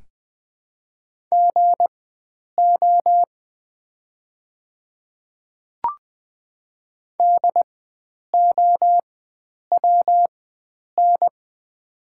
Morse code, heard as a keyed tone, about 20 wpm, spaced as for 9 wpm, 700 Hz. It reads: GO DOWN
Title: 4x Character Spacing - Top 100 Words - 20wpm